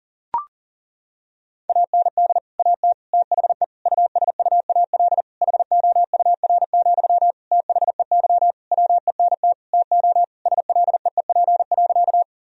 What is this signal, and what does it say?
Morse code, heard as a keyed tone, about 40 wpm, 700 Hz. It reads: AND AT THE USUAL HOUR, THEY WENT TO SLEEP.
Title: Sentences from Top 1000 Words - Review Part 2 - 40wpm